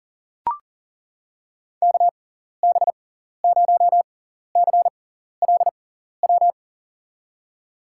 Morse code, heard as a keyed tone, about 40 wpm, spaced as for 14 wpm, 700 Hz. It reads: KB0CLW
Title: USA Call Signs - 6 Characters - 6x Character Spacing - 40wpm